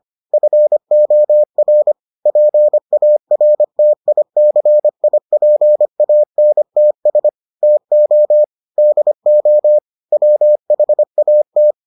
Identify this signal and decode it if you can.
Morse code, heard as a keyed tone, about 25 wpm, 600 Hz. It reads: FOR PARTICIPANTS TO DO WHAT